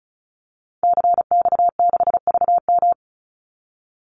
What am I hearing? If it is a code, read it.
Morse code, heard as a keyed tone, about 34 wpm, 700 Hz. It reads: CX6VM
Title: Call Signs Repeated 2x - 35wpm